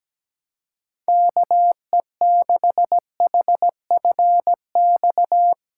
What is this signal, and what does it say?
Morse code, heard as a keyed tone, about 17 wpm, 700 Hz. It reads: KE6HFX
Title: USA Callsign DE Callsign - 6 Characters - 17wpm